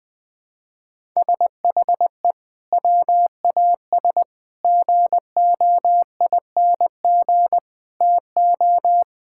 Morse code, heard as a keyed tone, about 20 wpm, 700 Hz. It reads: SHE WAS GOING TO
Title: Sentences with 4-Word Phrases - Spoken Beginning - 20wpm